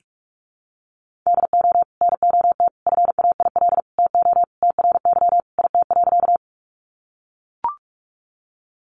Morse code, heard as a keyed tone, about 44 wpm, 700 Hz. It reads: DO NOT FAIL TO TRY IT.
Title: Sentences from Top 1100 Words 45wpm